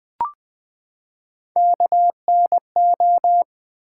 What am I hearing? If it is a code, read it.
Morse code, heard as a keyed tone, about 20 wpm, 700 Hz. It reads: KNO
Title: Build A Word - 3 Letters At A Time - 9-Letter Words - 20wpm